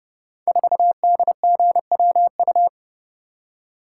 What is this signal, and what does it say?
Morse code, heard as a keyed tone, about 30 wpm, 700 Hz. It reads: 4DGWU
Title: Code Groups - 5 Letter:Numbers - 30wpm